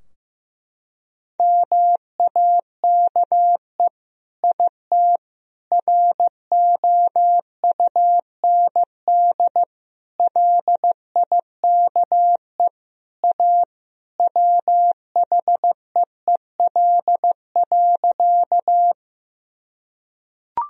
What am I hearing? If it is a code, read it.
Morse code, heard as a keyed tone, about 15 wpm, 700 Hz. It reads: MAKE IT ROUND LIKE A WHEEL.
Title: Sentences from Top 1000 Words - Review Part 3 - 15wpm